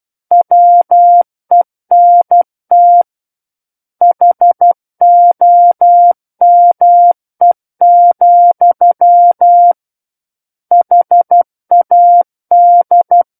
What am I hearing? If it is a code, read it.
Morse code, heard as a keyed tone, about 12 wpm, 700 Hz. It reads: WENT HOME, HAD
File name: Book One / 03-Chapter 3 - 12 WPM